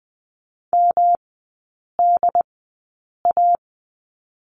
Morse code, heard as a keyed tone, about 20 wpm, spaced as for 8 wpm, 700 Hz. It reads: MDA
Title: 107 Lesson 18e - Letter m - Three Characters 20wpm